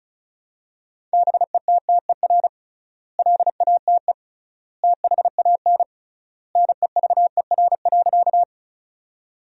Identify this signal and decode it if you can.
Morse code, heard as a keyed tone, about 36 wpm, 700 Hz. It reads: BETTER LATE THAN NEVER.
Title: 3x Word-Spacing - Sentences from Top 400 Words - 35wpm